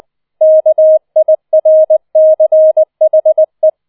Morse code, heard as a keyed tone, about 20 wpm, 630 Hz. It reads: KIRCHE